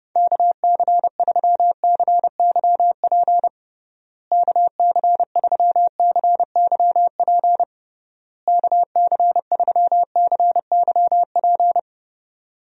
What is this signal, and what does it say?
Morse code, heard as a keyed tone, about 29 wpm, 700 Hz. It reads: KC3CYP KC3CYP KC3CYP
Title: USA Call Signs - 6 Characters - Repeated 3x - 30wpm